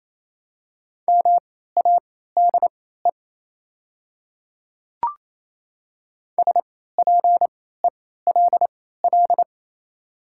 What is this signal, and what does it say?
Morse code, heard as a keyed tone, about 28 wpm, spaced as for 16 wpm, 700 Hz. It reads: MADE SPELL
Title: 3x Character Spacing - Top 300 Words - 28wpm